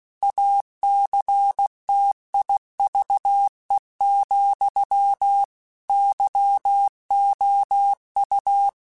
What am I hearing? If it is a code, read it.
Morse code, heard as a keyed tone, about 16 wpm, 800 Hz. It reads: ACTIVE, YOU
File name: Technical Subjects / Technical Subjects-mp3s / OR Gate @15